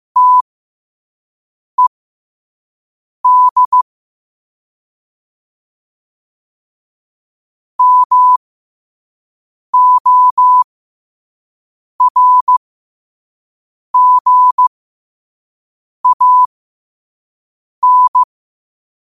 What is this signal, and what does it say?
Morse code, heard as a keyed tone, about 15 wpm, spaced as for 5 wpm, 1000 Hz. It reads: TED MORGAN
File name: qotdfiles / 5wpm